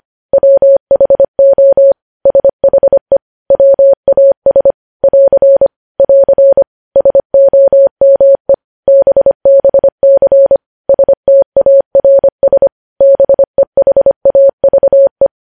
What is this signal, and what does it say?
Morse code, heard as a keyed tone, about 25 wpm, 550 Hz. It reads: WHO SHE WAS <AR> <AR> SOME BBC STARS BEHAVE
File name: podcast / News headlines-1x-25x25